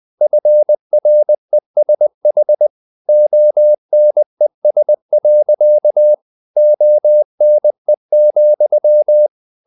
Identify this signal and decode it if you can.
Morse code, heard as a keyed tone, about 20 wpm, 600 Hz. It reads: FRESH ONES. ONE,
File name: the old man and the sea / the old man and the sea part 1 20wpm